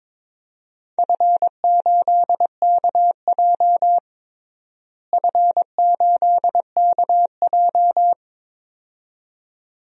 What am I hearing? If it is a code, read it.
Morse code, heard as a keyed tone, about 22 wpm, 700 Hz. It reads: F8KJ F8KJ
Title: DXCC-100 Call Signs - 4 Characters - Repeated 2x - 22wpm